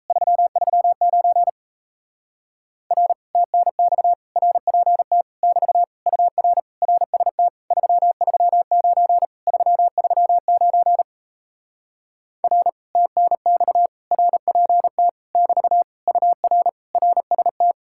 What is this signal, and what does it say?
Morse code, heard as a keyed tone, about 38 wpm, 700 Hz. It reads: 339 R TNX RPT <BT> UR RST 339 339 R TNX RPT <BT> UR RST